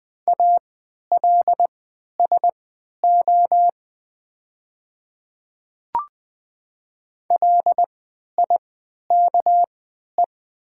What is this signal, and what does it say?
Morse code, heard as a keyed tone, about 20 wpm, spaced as for 11 wpm, 700 Hz. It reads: ALSO LIKE
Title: 3x Character Spacing - Top 300 Words - 20wpm